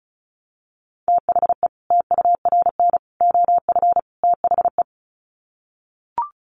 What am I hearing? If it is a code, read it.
Morse code, heard as a keyed tone, about 34 wpm, 700 Hz. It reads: THE TURN OF THE
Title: Sets of 4 Words 35wpm